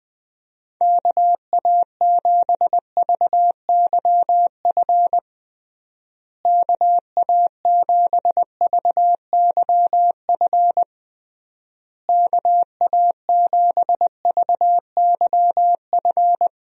Morse code, heard as a keyed tone, about 20 wpm, 700 Hz. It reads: KA7VYF KA7VYF KA7VYF
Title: USA Call Signs - 6 Characters - Repeated 3x - 20wpm